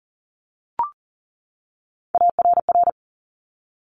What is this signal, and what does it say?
Morse code, heard as a keyed tone, about 40 wpm, 700 Hz. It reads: ARR
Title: Top Three Letter Combinations in Words 40wpm